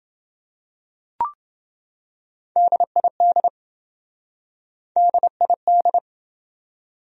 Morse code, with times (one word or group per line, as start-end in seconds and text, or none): 2.6-3.5 DID
5.0-6.0 DID